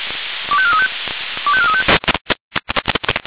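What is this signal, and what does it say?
Five-tone selective call (ZVEI), heard as quick runs of five tones, tones of about 70 ms. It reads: 25435 25435
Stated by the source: ZVEI five-tone selective call received on SDR.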